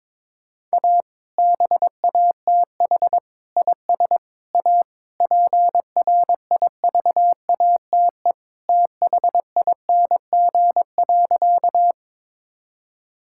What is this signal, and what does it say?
Morse code, heard as a keyed tone, about 23 wpm, 700 Hz. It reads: A BATH IS A PRIVATE THING.
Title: Sentences from Top 1600 Words 22wpm